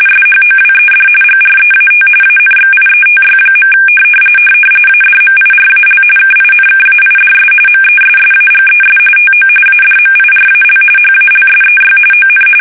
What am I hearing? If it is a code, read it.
RTTY, about 45.45 baud. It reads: SHOWED GREAT ANXIETY. ADMIRAL KINKAIDS MESSAGES TO ADMIRAL HALSEY ON
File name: fsk850 / battleshipnj